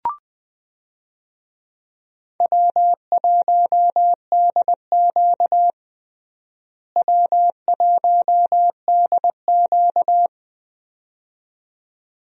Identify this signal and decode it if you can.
Morse code, heard as a keyed tone, about 20 wpm, 700 Hz. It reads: W1DQ W1DQ
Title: USA Call Signs - 4 Characters - Repeated 2x - 20wpm